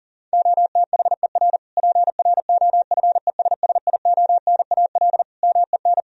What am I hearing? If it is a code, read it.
Morse code, heard as a keyed tone, about 37 wpm, 700 Hz. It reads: OTHER PROFESSIONAL MEN
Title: Sentences from Top 1000 Words - Review Part 4 - 40wpm